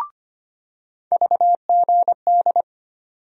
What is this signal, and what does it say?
Morse code, heard as a keyed tone, about 25 wpm, 700 Hz. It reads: VGD
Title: USA Call Sign Postfixes - 3 Letters 25wpm